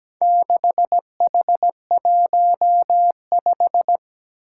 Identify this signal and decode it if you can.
Morse code, heard as a keyed tone, about 17 wpm, 700 Hz. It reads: 6H15
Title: Four Challenging Letter-Number Combinations 17wpm